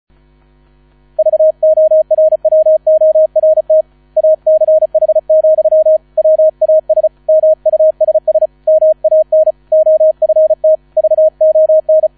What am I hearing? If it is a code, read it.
Morse code, heard as a keyed tone, about 34 wpm, 630 Hz, with radio noise. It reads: VORWORT ACH, WAS MUSS MAN OFT VON